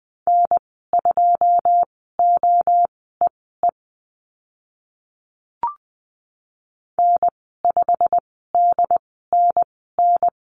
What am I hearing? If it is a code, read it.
Morse code, heard as a keyed tone, about 20 wpm, spaced as for 14 wpm, 700 Hz. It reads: N2OEE N5DNN
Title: 088 Lesson 15g - Number 5 - Call Signs 20wpm